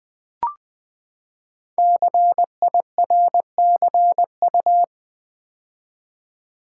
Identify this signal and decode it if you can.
Morse code, heard as a keyed tone, about 20 wpm, 700 Hz. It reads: CIRCU
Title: Build A Word - 5 Letters At A Time - 12-Letter Words - 20wpm